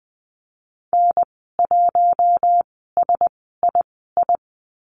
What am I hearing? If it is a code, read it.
Morse code, heard as a keyed tone, about 20 wpm, spaced as for 14 wpm, 700 Hz. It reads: N1SII
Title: 042 Lesson 8g - Number 1 - Call Signs 20wpm